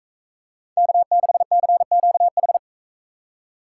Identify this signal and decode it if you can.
Morse code, heard as a keyed tone, about 44 wpm, 700 Hz. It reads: K6CQH